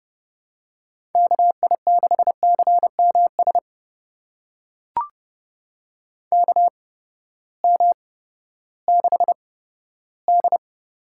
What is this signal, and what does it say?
Morse code, heard as a keyed tone, about 30 wpm, spaced as for 15 wpm, 700 Hz. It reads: KI6CMS KM6D
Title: USA Call Signs - 6 Characters - 8x Character Spacing - 30wpm